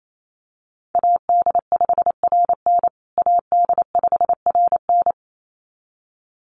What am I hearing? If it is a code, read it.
Morse code, heard as a keyed tone, about 28 wpm, 700 Hz. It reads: AD5RN AD5RN
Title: USA Call Signs Repeated 2x 28wpm